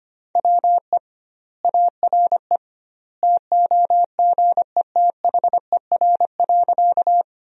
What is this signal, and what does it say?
Morse code, heard as a keyed tone, about 25 wpm, 700 Hz. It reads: WE ARE TOGETHER.
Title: 2x Word-Spacing - Sentences from Top 300 Words - 25wpm